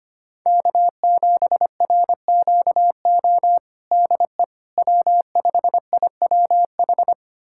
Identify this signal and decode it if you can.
Morse code, heard as a keyed tone, about 25 wpm, 700 Hz. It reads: K7RQO DE W5IWH
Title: USA Callsign DE Callsign - 5 Characters - 25wpm